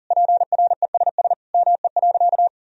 Morse code, heard as a keyed tone, about 40 wpm, 700 Hz. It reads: PRESS ME.
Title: Sentences from Top 1000 Words - Review Part 1 - 40wpm